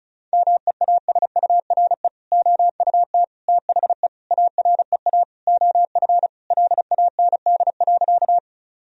Morse code, heard as a keyed tone, about 34 wpm, 700 Hz. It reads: MEASURE OUT THE AREA OF LAND.
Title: Sentences from Top 1000 Words - Review Part 2 - 35wpm